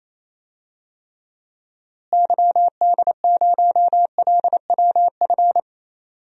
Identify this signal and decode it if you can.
Morse code, heard as a keyed tone, about 28 wpm, 700 Hz. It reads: YD0LWF
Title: DXCC-100 Call Signs - 6 Characters - Repeated 3x - 28wpm